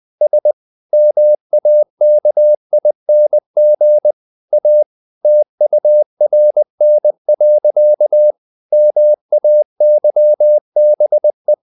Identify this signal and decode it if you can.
Morse code, heard as a keyed tone, about 20 wpm, 600 Hz. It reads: S MAKING A TURN. MAYBE
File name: the old man and the sea / the old man and the sea part 1 20wpm